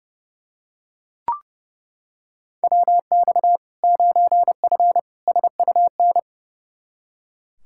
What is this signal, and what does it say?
Morse code, heard as a keyed tone, about 30 wpm, 700 Hz. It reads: WX 9F SUN
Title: Practice QSO - Exchange 2 - Info - 30wpm